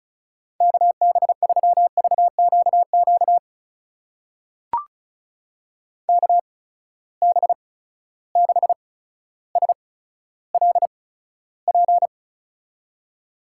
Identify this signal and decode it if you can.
Morse code, heard as a keyed tone, about 36 wpm, spaced as for 15 wpm, 700 Hz. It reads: KB3VQQ KB6SLP
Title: USA Call Signs - 6 Characters - 8x Character Spacing - 35wpm